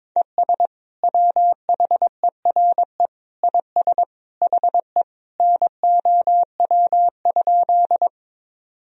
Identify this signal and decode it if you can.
Morse code, heard as a keyed tone, about 21 wpm, 700 Hz. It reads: ES WHERE IS HE NOW?
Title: Sentences from Top 200 Words and Abbreviations 22wpm